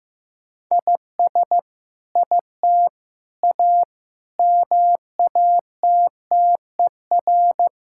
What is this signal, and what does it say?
Morse code, heard as a keyed tone, about 16 wpm, 700 Hz. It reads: IS IT A MATTER
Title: Sentences from Top 1000 Words - Review Part 5 - 15wpm